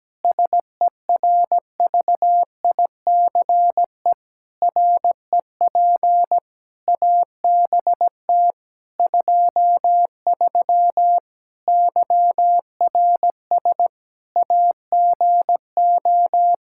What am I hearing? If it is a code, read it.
Morse code, heard as a keyed tone, about 17 wpm, 700 Hz. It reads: SERVICE REP ABT 23 YRS AGO
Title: K1USN SST QSO Practice 17wpm